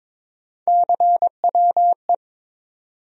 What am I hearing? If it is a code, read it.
Morse code, heard as a keyed tone, about 22 wpm, 700 Hz. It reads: CWE
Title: USA Call Sign Postfixes - 3 Letters 22wpm